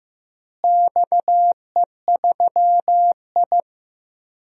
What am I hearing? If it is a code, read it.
Morse code, heard as a keyed tone, about 15 wpm, 700 Hz. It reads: XE3I
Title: DXCC-100 Call Signs - 4 Characters - Repeated 3x - 15wpm